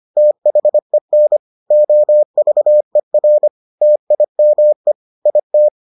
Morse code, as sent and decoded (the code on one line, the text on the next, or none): - .... -. / --- ...- . .-. / - .. -- . / .. -
THEN OVER TIME IT